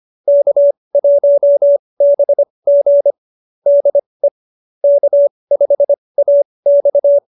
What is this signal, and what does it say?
Morse code, heard as a keyed tone, about 25 wpm, spaced as for 20 wpm, 590 Hz. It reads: K1BG DE K5AX